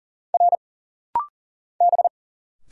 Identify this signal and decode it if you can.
Morse code, heard as a keyed tone, about 40 wpm, 700 Hz. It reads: R B
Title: Single Letters - Rapid-Fire 40wpm